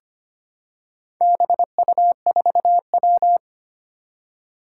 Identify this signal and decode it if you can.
Morse code, heard as a keyed tone, about 25 wpm, 700 Hz. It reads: BU4W
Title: Four Challenging Letter-Number Combinations 25wpm